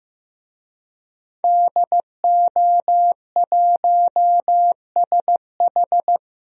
Morse code, heard as a keyed tone, about 15 wpm, 700 Hz. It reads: DO1SH
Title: DXCC-100 Call Signs - 5 Characters - 15wpm